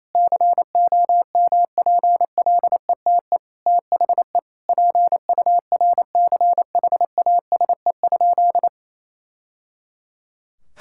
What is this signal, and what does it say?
Morse code, heard as a keyed tone, about 28 wpm, 700 Hz. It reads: COMPLETE THE PURCHASE?
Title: Sentences from Top 1000 Words - Review Part 5 - 28wpm